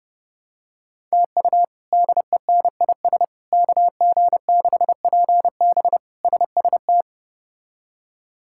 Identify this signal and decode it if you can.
Morse code, heard as a keyed tone, about 30 wpm, 700 Hz. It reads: TU DENIS KG6PB SST
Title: K1USN SST QSO Practice - 30wpm